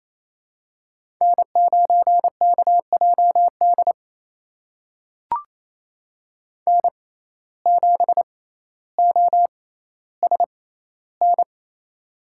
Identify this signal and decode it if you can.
Morse code, heard as a keyed tone, about 27 wpm, spaced as for 14 wpm, 700 Hz. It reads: N9KJD N7OSN
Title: USA Call Signs - 5 Characters - 6x Character Spacing - 28wpm